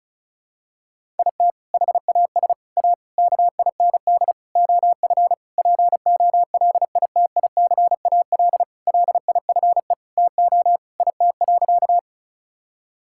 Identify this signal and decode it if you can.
Morse code, heard as a keyed tone, about 36 wpm, 700 Hz. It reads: IT HAS A KIND OF POLITICAL LIFE TO IT.
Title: Sentences from Top 1000 Words - Review Part 5 - 35wpm